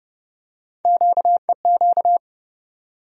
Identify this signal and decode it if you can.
Morse code, heard as a keyed tone, about 30 wpm, 700 Hz. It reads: QEQ
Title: USA Call Sign Postfixes - 3 Letters 30wpm